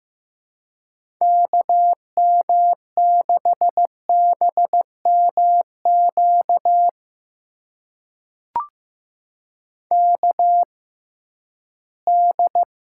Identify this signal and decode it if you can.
Morse code, heard as a keyed tone, about 15 wpm, spaced as for 11 wpm, 700 Hz. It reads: KM6BMQ KD
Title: USA Call Signs - 6 Characters - 6x Character Spacing - 15wpm